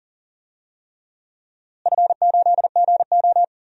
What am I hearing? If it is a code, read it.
Morse code, heard as a keyed tone, about 38 wpm, 700 Hz. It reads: F8GO
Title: DXCC-100 Call Signs - 4 Characters - Repeated 3x - 40wpm